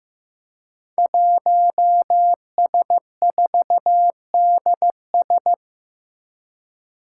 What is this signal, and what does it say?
Morse code, heard as a keyed tone, about 15 wpm, 700 Hz. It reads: 1S4DS
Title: Five Challenging Letter-Number Combinations 15wpm